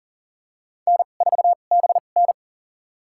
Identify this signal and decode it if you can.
Morse code, heard as a keyed tone, about 40 wpm, spaced as for 29 wpm, 700 Hz. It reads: N4BN
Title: USA Call Signs - 4 Characters - 2x Character Spacing - 40wpm